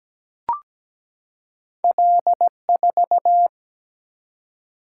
Morse code, heard as a keyed tone, about 17 wpm, 700 Hz. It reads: L4
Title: Single-Letter Single-Number 17wpm